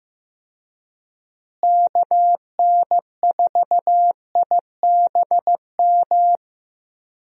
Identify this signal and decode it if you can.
Morse code, heard as a keyed tone, about 15 wpm, 700 Hz. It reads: KN4IBM
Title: USA Call Signs - 6 Characters - Repeated 2x - 15wpm